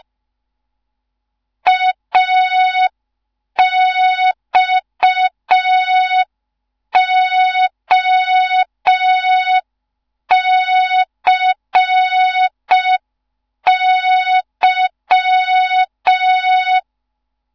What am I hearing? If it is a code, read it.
Morse code, heard as a keyed tone, about 5 wpm, 3700 Hz, with noise. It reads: AXOCY
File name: Morze-HA2MN / hang / 15 GYAKORLAT